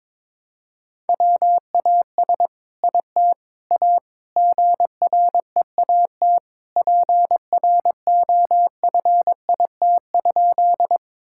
Morse code, heard as a keyed tone, about 22 wpm, 700 Hz. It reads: WAS IT A GREAT PROFIT?
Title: Sentences from Top 1000 Words - Review Part 5 - 22wpm